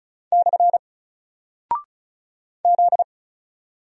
Morse code, heard as a keyed tone, about 36 wpm, 700 Hz. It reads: / Z
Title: Single Character (Letter:Number:Punctuation:Prosign) 35wpm